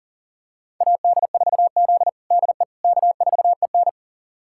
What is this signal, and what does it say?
Morse code, heard as a keyed tone, about 40 wpm, 700 Hz. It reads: AD4Z DE K4EN
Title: USA Callsign DE Callsign - 4 Characters - 40wpm